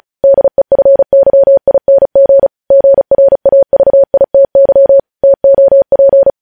Morse code, heard as a keyed tone, about 36 wpm, 550 Hz. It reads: DEFYING GRAVITY TOP
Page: News Headlines in Morse Code at 35 WPM